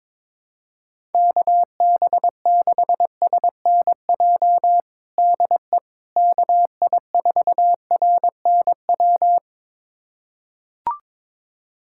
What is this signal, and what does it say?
Morse code, heard as a keyed tone, about 22 wpm, 700 Hz. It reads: KB6SNJ DE KI4RNW
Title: USA Callsign DE Callsign - 6 Characters - 22wpm